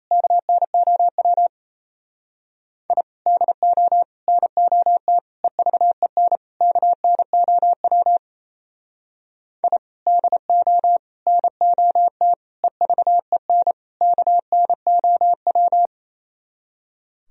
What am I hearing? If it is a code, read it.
Morse code, heard as a keyed tone, about 34 wpm, 700 Hz. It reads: KNOW I DO NOT EVEN KNOW I DO NOT EVEN KNOW
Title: Speed-Racing - Sets of 5 Words 28wpm